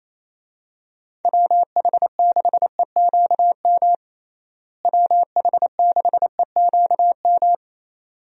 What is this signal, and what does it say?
Morse code, heard as a keyed tone, about 28 wpm, 700 Hz. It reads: WH6EQM WH6EQM
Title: USA Call Signs - 6 Characters - Repeated 2x - 28wpm